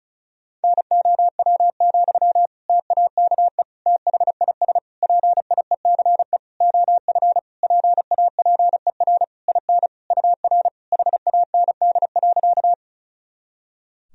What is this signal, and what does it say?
Morse code, heard as a keyed tone, about 34 wpm, 700 Hz. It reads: NOW, TAKE THIS PIECE OF PAPER IN UR HAND.
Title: Sentences from Top 400 Words and Abbreviations 35wpm